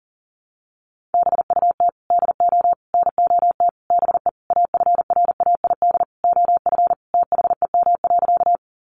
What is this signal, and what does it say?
Morse code, heard as a keyed tone, about 40 wpm, 700 Hz. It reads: BUT DO NOT BE AFRAID OF THEM.
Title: Sentences from Top 1500 Words 40wpm